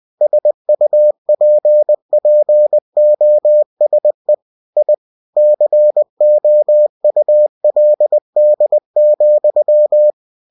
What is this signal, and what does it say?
Morse code, heard as a keyed tone, about 20 wpm, 600 Hz. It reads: SUPPOSE I COULD,